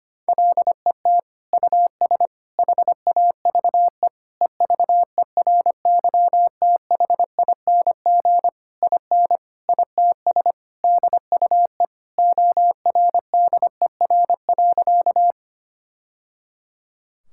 Morse code, heard as a keyed tone, about 25 wpm, 700 Hz. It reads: LET US HAVE EVERYTHING IN ITS DUE ORDER.
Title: Sentences from Top 1000 Words - Review Part 6 - 25wpm